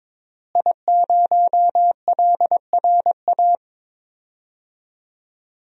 Morse code, heard as a keyed tone, about 22 wpm, 700 Hz. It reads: I0LRA
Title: DXCC-100 Call Signs - 5 Characters - Repeated 3x - 22wpm